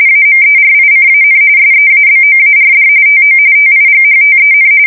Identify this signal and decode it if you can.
RTTY, about 45.45 baud. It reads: AMERICAN SEABOARD TO TIGHTEN
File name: fsk170 / usnavy2